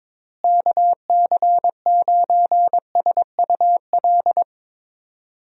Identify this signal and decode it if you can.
Morse code, heard as a keyed tone, about 22 wpm, 700 Hz. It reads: KC9SUL
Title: USA Call Signs - 6 Characters - Repeated 2x - 22wpm